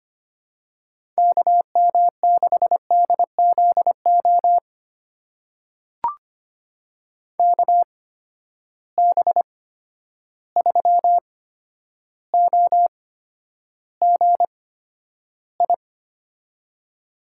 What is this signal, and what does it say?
Morse code, heard as a keyed tone, about 25 wpm, spaced as for 11 wpm, 700 Hz. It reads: KM6DZO KB3OGI
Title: USA Call Signs - 6 Characters - 8x Character Spacing - 25wpm